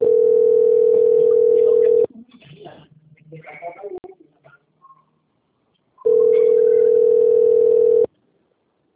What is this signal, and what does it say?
Ringback tone (North America), 440 + 480 Hz, with speech.
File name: out- - - - - .19452